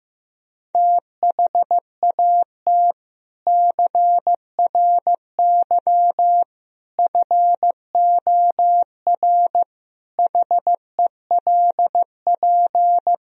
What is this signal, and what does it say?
Morse code, heard as a keyed tone, about 15 wpm, 700 Hz. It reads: THAT CRY FOR HELP
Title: Sentences from Top 1000 Words - Review Part 3 - 15wpm